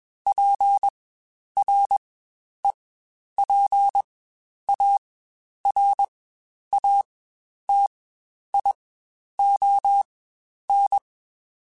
Morse code, heard as a keyed tone, about 21 wpm, spaced as for 10 wpm, 800 Hz. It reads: PREPARATION
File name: Quotes Spaced / Quotes Spaced-mp3s / Persperation sp @20